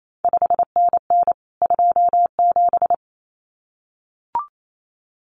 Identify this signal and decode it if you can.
Morse code, heard as a keyed tone, about 28 wpm, 700 Hz. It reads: 5NN 27
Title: CQ WW DX Contest Exchange Encore - 28wpm